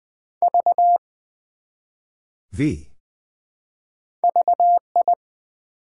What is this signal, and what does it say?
Morse code, heard as a keyed tone, about 20 wpm, 700 Hz. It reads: V VI
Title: Build A Word - 1 Letter At A Time - 4-Letter Words - 20wpm